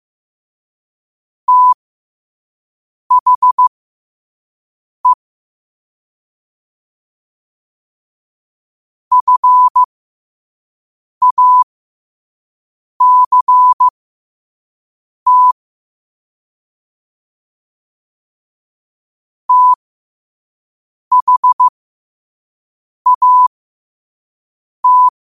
Morse code, heard as a keyed tone, about 15 wpm, spaced as for 5 wpm, 1000 Hz. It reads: THE FACT THAT
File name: qotdfiles / 5wpm